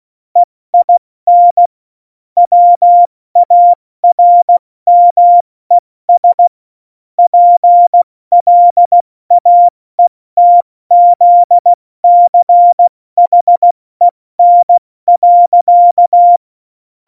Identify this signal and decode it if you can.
Morse code, heard as a keyed tone, about 16 wpm, spaced as for 14 wpm, 700 Hz. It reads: EIN WARMES PLAETZCHEN.
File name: Morsekurs / Witze / Witz 07. Januar 2020, 70 F80